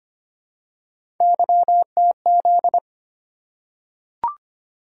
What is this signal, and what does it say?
Morse code, heard as a keyed tone, about 25 wpm, 700 Hz. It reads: YTZ
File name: Three Letter-Number Combinations - 25wpm